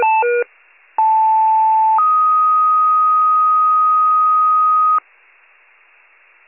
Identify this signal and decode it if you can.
Two-tone page, heard as a one-second tone, then a three-second tone, about 880 then 1250 Hz. It is heard from 1.0 to 5.0 s.
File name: wcalert